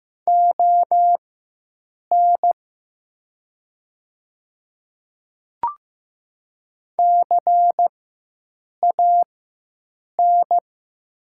Morse code, heard as a keyed tone, about 15 wpm, spaced as for 7 wpm, 700 Hz. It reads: ON CAN
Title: 4x Character Spacing - Top 100 Words - 15wpm